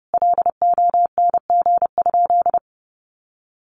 Morse code, heard as a keyed tone, about 30 wpm, 700 Hz. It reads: LONG?
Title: Sentences from Top 1200 Words 30wpm